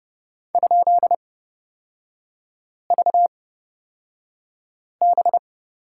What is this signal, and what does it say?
Morse code, heard as a keyed tone, about 31 wpm, spaced as for 5 wpm, 700 Hz. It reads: ?VB
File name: Lesson 36e - BT - Three Characters 30wpm